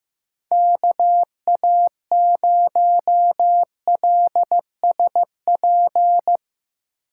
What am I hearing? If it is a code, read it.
Morse code, heard as a keyed tone, about 15 wpm, 700 Hz. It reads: KA0LSP
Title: USA Call Signs - 6x Character Spacing - 15wpm